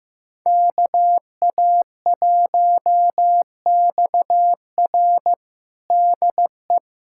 Morse code, heard as a keyed tone, about 16 wpm, 700 Hz. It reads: KA1XR DE
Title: USA Callsign DE Callsign - 5 Characters - 15wpm